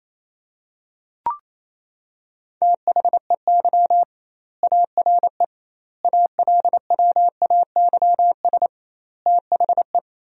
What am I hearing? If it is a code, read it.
Morse code, heard as a keyed tone, about 28 wpm, 700 Hz. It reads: THEY ARE ALWAYS THE
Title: 2x Word-Spacing - Sentences from Top 300 Words - 28wpm